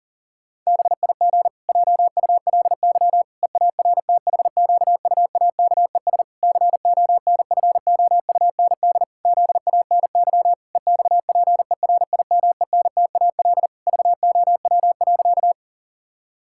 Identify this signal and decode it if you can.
Morse code, heard as a keyed tone, about 40 wpm, 700 Hz. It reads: BIG JULY EARTHQUAKES CONFOUND ZANY EXPERIMENTAL VOW.
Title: Pangram Challenge v2 - 40wpm